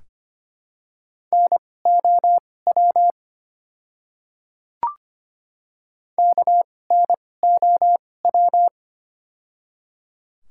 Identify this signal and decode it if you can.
Morse code, heard as a keyed tone, about 25 wpm, spaced as for 18 wpm, 700 Hz. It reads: NOW KNOW
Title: 2x Character Spacing - Top 100 Words - 25wpm